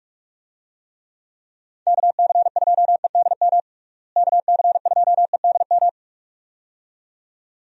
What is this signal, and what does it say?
Morse code, heard as a keyed tone, about 46 wpm, 700 Hz. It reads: KC2EDM KC2EDM
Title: USA Call Signs - 6 Characters - Repeated 2x - 45wpm